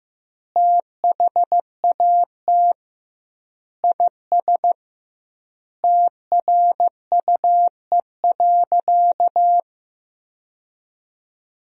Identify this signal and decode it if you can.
Morse code, heard as a keyed tone, about 15 wpm, 700 Hz. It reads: THAT IS TRUE.
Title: 2x Word-Spacing - Sentences from Top 500 Words - 15wpm